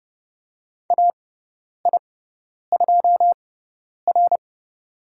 Morse code, heard as a keyed tone, about 30 wpm, spaced as for 10 wpm, 700 Hz. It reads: AI2R
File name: Lesson 16g - Letter c - Call Signs 30wpm